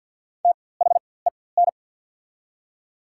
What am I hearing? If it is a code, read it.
Morse code, heard as a keyed tone, about 49 wpm, spaced as for 23 wpm, 690 Hz. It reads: THEN
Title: 4x Character Spacing - Top 100 Words - 50wpm